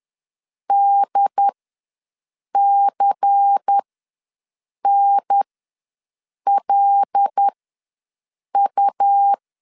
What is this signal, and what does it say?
Morse code, heard as a keyed tone, about 10 wpm, spaced as for 6 wpm, 800 Hz. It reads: DCNLU
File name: LEZIONI / 6 PKX0 / 6 PK ws 10wpm 3745